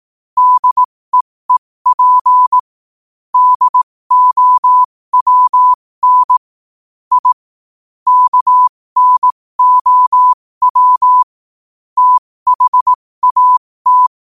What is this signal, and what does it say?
Morse code, heard as a keyed tone, about 18 wpm, spaced as for 15 wpm, 1000 Hz. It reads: DEEP DOWN I KNOW THAT
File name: qotdfiles / 15wpm